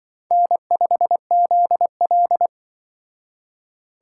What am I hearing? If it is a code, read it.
Morse code, heard as a keyed tone, about 25 wpm, 700 Hz. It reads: N5ZL
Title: Speed Racing - USA 1x2 Call Signs 20wpm